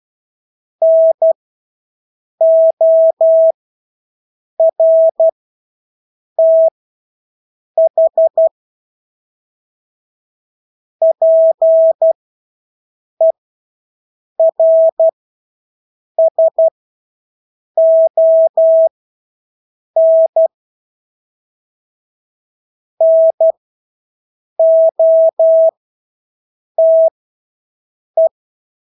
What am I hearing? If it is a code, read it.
Morse code, heard as a keyed tone, about 12 wpm, spaced as for 6 wpm, 650 Hz. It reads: NORTH PERSON NOTE
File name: Engelske ord 12-6 wpm